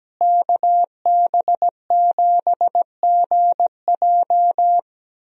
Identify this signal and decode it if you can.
Morse code, heard as a keyed tone, about 17 wpm, 700 Hz. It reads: KB7GJ
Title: USA Call Signs - 5 Characters - Repeated 3x - 17wpm